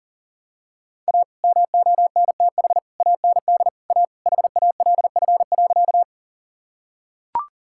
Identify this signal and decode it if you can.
Morse code, heard as a keyed tone, about 37 wpm, 700 Hz. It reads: A MONTH AND A HALF.
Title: Sentences from Top 1000 Words - Review Part 5 - 40wpm